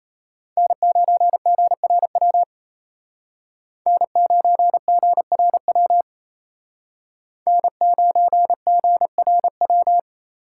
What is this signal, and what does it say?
Morse code, heard as a keyed tone, about 32 wpm, 700 Hz. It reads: N9GRW N9GRW N9GRW
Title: Speed Racing - USA 1x3 Call Signs 28wpm